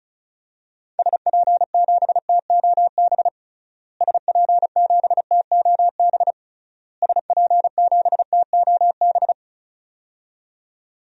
Morse code, heard as a keyed tone, about 36 wpm, 700 Hz. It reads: SP7TOB SP7TOB SP7TOB
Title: DXCC-100 Call Signs - 6 Characters - Repeated 3x - 35wpm